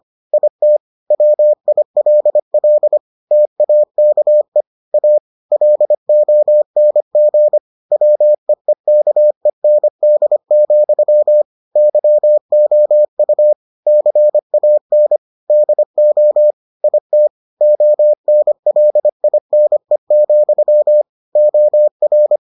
Morse code, heard as a keyed tone, about 25 wpm, 600 Hz. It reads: IT WILL TAKE A LONG WEEKEND, YOU CAN DO IT ONLINE, OR